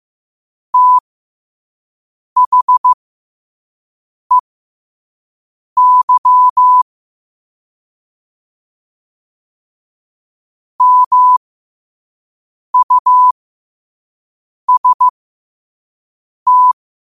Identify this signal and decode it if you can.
Morse code, heard as a keyed tone, about 15 wpm, spaced as for 5 wpm, 1000 Hz. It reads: THEY MUST
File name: qotdfiles / 5wpm